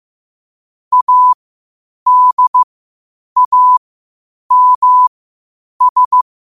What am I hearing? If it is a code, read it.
Morse code, heard as a keyed tone, about 15 wpm, spaced as for 8 wpm, 1000 Hz. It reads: ADAMS
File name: qotdfiles / 8wpm